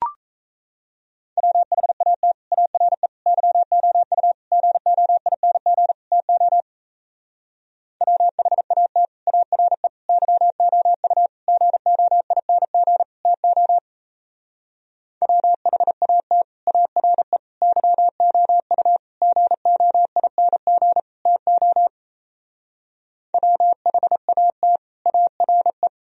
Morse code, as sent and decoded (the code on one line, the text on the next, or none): .-- .... .- - / .- .-. / -.-- --- ..- / --. --- .. -. --. / - --- / .-- .... .- - / .- .-. / -.-- --- ..- / --. --- .. -. --. / - --- / .-- .... .- - / .- .-. / -.-- --- ..- / --. --- .. -. --. / - --- / .-- .... .- - / .- .-. .
WHAT ARE YOU GOING TO WHAT ARE YOU GOING TO WHAT ARE YOU GOING TO WHAT ARE